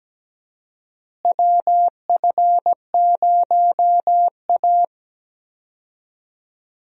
Morse code, heard as a keyed tone, about 17 wpm, 700 Hz. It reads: WF0A
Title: USA Call Signs - 4 Characters - Repeated 2x - 17wpm